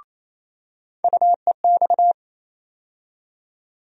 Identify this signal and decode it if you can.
Morse code, heard as a keyed tone, about 28 wpm, 700 Hz. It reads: UEX
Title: USA Call Sign Postfixes - 3 Letters 28wpm